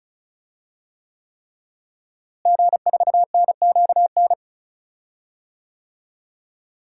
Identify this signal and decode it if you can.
Morse code, heard as a keyed tone, about 36 wpm, 700 Hz. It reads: G4NQN
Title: DXCC-100 Call Signs - 5 Characters - Repeated 3x - 35wpm